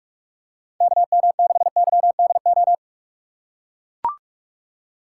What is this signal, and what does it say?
Morse code, heard as a keyed tone, about 42 wpm, 700 Hz. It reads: KM6YDO
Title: USA Call Signs - 6 Characters - 4x Character Spacing - 45wpm